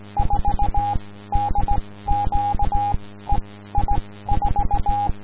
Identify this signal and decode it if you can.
Morse code, heard as a keyed tone, about 17 wpm, spaced as for 14 wpm, 820 Hz, with noise. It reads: 4DQ EI4
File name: ei4dq / aurora 15 march 2012 / DG5CST